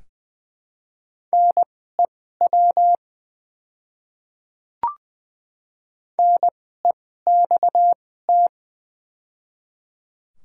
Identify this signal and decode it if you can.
Morse code, heard as a keyed tone, about 20 wpm, spaced as for 14 wpm, 700 Hz. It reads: NEW NEXT
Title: 2x Character Spacing - Top 300 Words - 20wpm